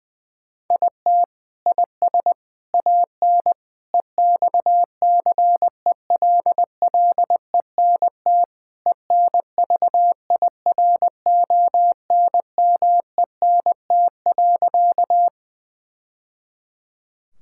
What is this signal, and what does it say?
Morse code, heard as a keyed tone, about 20 wpm, 700 Hz. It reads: IT IS AN EXCELLENT ENVIRONMENT.
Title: Sentences from Top 1000 Words - Review Part 5 - 20wpm